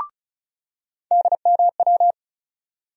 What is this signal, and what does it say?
Morse code, heard as a keyed tone, about 36 wpm, 700 Hz. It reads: DMW
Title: USA Call Sign Postfixes - 3 Letters 35wpm